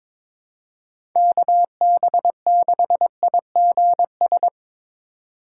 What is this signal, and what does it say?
Morse code, heard as a keyed tone, about 23 wpm, 700 Hz. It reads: KB6IGS
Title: USA Call Signs - 6 Characters - 2x Character Spacing - 22wpm